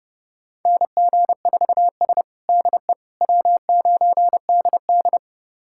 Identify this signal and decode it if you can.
Morse code, heard as a keyed tone, about 30 wpm, 700 Hz. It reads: NG4S DE W9DD
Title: USA Callsign DE Callsign - 4 Characters - 30wpm